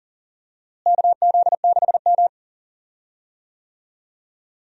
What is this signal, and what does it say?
Morse code, heard as a keyed tone, about 40 wpm, 700 Hz. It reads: KZ6M
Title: USA Call Signs - 4 Characters - Repeated 3x - 40wpm